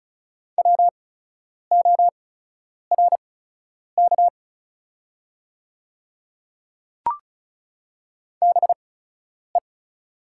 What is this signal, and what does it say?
Morse code, heard as a keyed tone, about 34 wpm, spaced as for 10 wpm, 690 Hz. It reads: WORK BE
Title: 8x Character Spacing - Top 200 Words - 35wpm